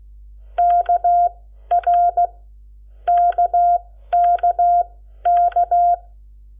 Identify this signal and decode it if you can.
Morse code, heard as a keyed tone, about 16 wpm, spaced as for 11 wpm, 700 Hz. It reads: KRKKK
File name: Lecciones / CLASE Nº7